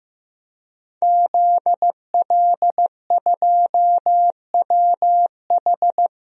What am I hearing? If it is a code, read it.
Morse code, heard as a keyed tone, about 15 wpm, 700 Hz. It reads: ZL2WH
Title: DXCC-100 Call Signs - 5 Characters - 15wpm